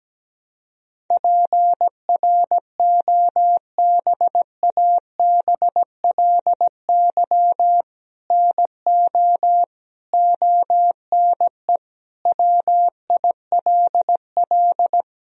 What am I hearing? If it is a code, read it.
Morse code, heard as a keyed tone, about 17 wpm, 700 Hz. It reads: PROBABLY NO ONE WILL
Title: Sentences from Top 1000 Words - Review Part 4 - 17wpm